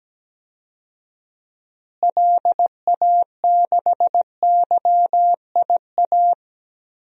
Morse code, heard as a keyed tone, about 17 wpm, 700 Hz. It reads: LA6YIA
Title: DXCC-100 Call Signs - 6 Characters - Repeated 3x - 17wpm